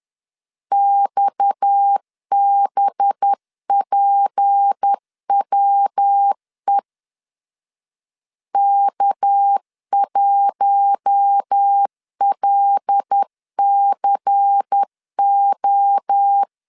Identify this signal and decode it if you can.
Morse code, heard as a keyed tone, about 10 wpm, 800 Hz. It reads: XBPWE K1LCO
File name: LEZIONI / 6 PKX0 / 6 PKX 10wpm 2426